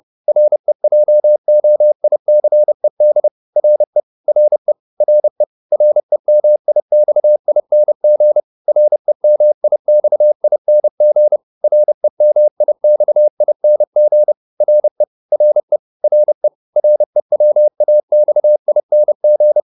Morse code, heard as a keyed tone, about 30 wpm, 600 Hz. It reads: REJOICED RE RE RE REMIXING REMIXING REMIXING RE RE RE REWAXING